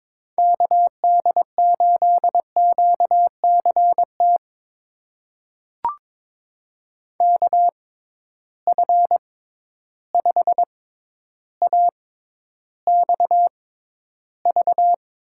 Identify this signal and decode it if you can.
Morse code, heard as a keyed tone, about 22 wpm, spaced as for 11 wpm, 700 Hz. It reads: KD8QCT KF5AXV
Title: USA Call Signs - 6 Characters - 6x Character Spacing - 22wpm